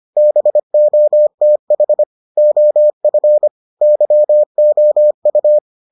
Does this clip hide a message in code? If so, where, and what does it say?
Yes: Morse code from 0.2 to 5.6 s, heard as a keyed tone, about 25 wpm, 600 Hz. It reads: BOTH OF YOU